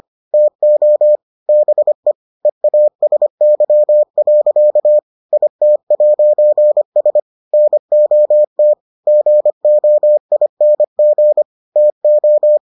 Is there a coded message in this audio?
Yes: Morse code from 0.3 to 12.6 s, heard as a keyed tone, about 25 wpm, 600 Hz. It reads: TO BE EASY. IT'S NOT GOING TO